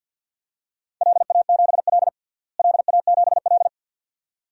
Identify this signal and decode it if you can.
Morse code, heard as a keyed tone, about 52 wpm, 700 Hz. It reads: LA7L LA7L
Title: DXCC-100 Call Signs - 4 Characters - Repeated 2x - 50wpm